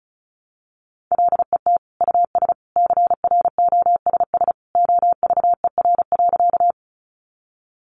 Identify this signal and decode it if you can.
Morse code, heard as a keyed tone, about 34 wpm, 700 Hz. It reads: LET US CROSS OVER.
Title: Sentences from Top 300 Words 35wpm